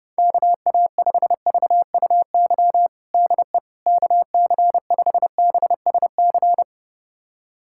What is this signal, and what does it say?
Morse code, heard as a keyed tone, about 32 wpm, 700 Hz. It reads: KA5VUY DE KC5BSC
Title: USA Callsign DE Callsign - 6 Characters - 30wpm